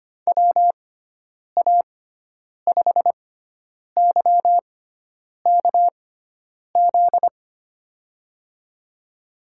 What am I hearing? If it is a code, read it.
Morse code, heard as a keyed tone, about 25 wpm, spaced as for 9 wpm, 700 Hz. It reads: WA5YKZ
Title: USA Call Signs - 6 Characters - 6x Character Spacing - 25wpm